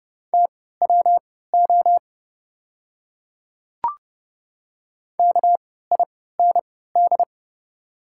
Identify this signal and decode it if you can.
Morse code, heard as a keyed tone, about 30 wpm, spaced as for 17 wpm, 700 Hz. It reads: TWO KIND
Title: 3x Character Spacing - Top 300 Words - 30wpm